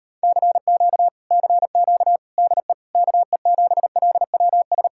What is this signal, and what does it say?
Morse code, heard as a keyed tone, about 33 wpm, 700 Hz. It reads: CQ CQ DE KE7LWS